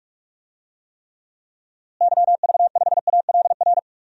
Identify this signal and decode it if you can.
Morse code, heard as a keyed tone, about 42 wpm, 700 Hz. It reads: YV5ALR
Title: DXCC-100 Call Signs - 6 Characters - Repeated 2x - 45wpm